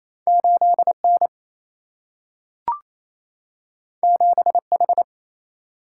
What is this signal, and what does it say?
Morse code, heard as a keyed tone, about 28 wpm, 700 Hz. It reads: 8 N 7 H
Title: Single-Letter Single-Number 28wpm